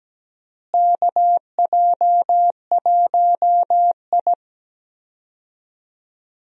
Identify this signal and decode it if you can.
Morse code, heard as a keyed tone, about 17 wpm, 700 Hz. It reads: KJ1I
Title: USA Call Signs - 4 Characters - Repeated 3x - 17wpm